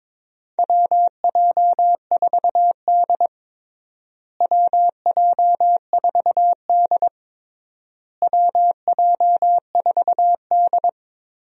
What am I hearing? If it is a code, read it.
Morse code, heard as a keyed tone, about 22 wpm, 700 Hz. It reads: WJ4D WJ4D WJ4D
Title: USA Call Signs - 4 Characters - Repeated 3x - 22wpm